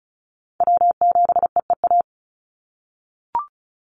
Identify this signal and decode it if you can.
Morse code, heard as a keyed tone, about 36 wpm, 700 Hz. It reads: W7EEA
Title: USA Call Signs 35wpm